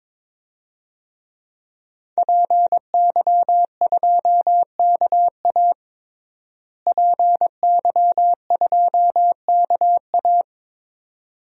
Morse code, heard as a keyed tone, about 22 wpm, 700 Hz. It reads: PY2KA PY2KA
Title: DXCC-100 Call Signs - 5 Characters - Repeated 2x - 22wpm